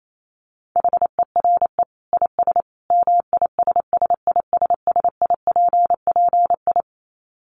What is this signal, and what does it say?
Morse code, heard as a keyed tone, about 28 wpm, 700 Hz. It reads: HERE IS MISSISSIPPI
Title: Dit Challenge 28wpm